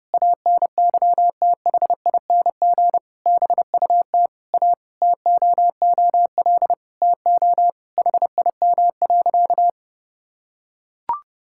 Morse code, heard as a keyed tone, about 29 wpm, 700 Hz. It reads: ANYTHING BUT A TOOL TO HIM.
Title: Sentences from Top 1000 Words - Review Part 6 - 30wpm